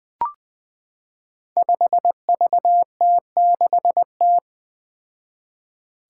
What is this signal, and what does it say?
Morse code, heard as a keyed tone, about 20 wpm, 700 Hz. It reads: 5VT6T
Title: Five Challenging Letter-Number Combinations 20wpm